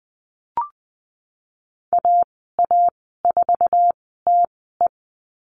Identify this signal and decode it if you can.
Morse code, heard as a keyed tone, about 20 wpm, spaced as for 14 wpm, 700 Hz. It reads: AA4TE
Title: 088 Lesson 15g - Number 5 - Call Signs 20wpm